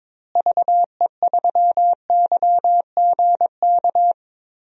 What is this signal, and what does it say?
Morse code, heard as a keyed tone, about 22 wpm, 700 Hz. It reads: VE3YGK
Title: DXCC-100 Call Signs - 6 Characters - Repeated 3x - 22wpm